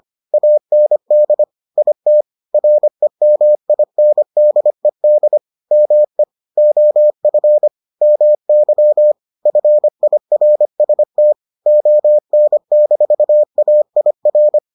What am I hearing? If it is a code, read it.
Morse code, heard as a keyed tone, about 25 wpm, 600 Hz. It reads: AND IT REMINDED ME OF MY FIRST ON-AIR